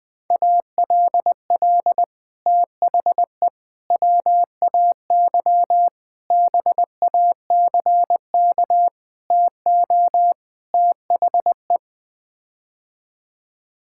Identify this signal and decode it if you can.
Morse code, heard as a keyed tone, about 20 wpm, 700 Hz. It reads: ALL THE WAY BACK TO THE
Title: Sets of 6 Words - 20wpm